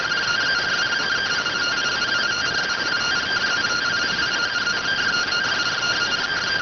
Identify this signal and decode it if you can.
Two-tone teleprinter signal, about 170 Hz shift.